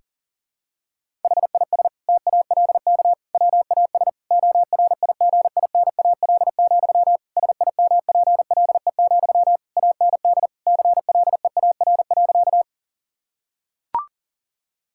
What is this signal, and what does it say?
Morse code, heard as a keyed tone, about 40 wpm, 700 Hz. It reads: HIS TALK WAS ORIGINAL, SIMPLE, AND CLEAR.
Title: Sentences from Top 1000 Words - Review Part 6 - 40wpm